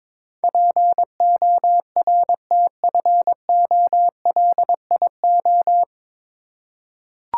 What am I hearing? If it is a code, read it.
Morse code, heard as a keyed tone, about 23 wpm, 700 Hz. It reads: PORTFOLIO
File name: 9 Letter Words - First 3 Letters Given - 22wpm